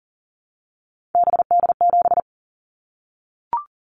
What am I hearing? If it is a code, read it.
Morse code, heard as a keyed tone, about 42 wpm, 700 Hz. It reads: BD7
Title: Callsign Prefixes for Top 100 DXCC - 40wpm